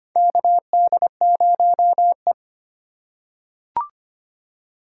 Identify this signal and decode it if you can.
Morse code, heard as a keyed tone, about 25 wpm, 700 Hz. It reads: KD0E
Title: USA Call Signs - 4 Characters - 4x Character Spacing - 25wpm